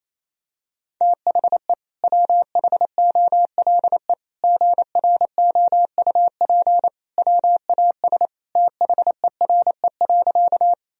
Morse code, heard as a keyed tone, about 28 wpm, 700 Hz. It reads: THE WHOLE GROUP WAS THERE.
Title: Sentences from Top 1000 Words - Review Part 1 - 28wpm